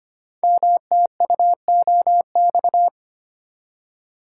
Morse code, heard as a keyed tone, about 25 wpm, 700 Hz. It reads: MTUOX
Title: Code Groups - 5 Letters - 25wpm